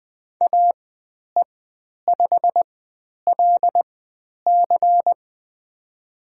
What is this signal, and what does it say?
Morse code, heard as a keyed tone, about 20 wpm, spaced as for 10 wpm, 700 Hz. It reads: AE5LC
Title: 102 Lesson 17g - Letter u - Call Signs 20wpm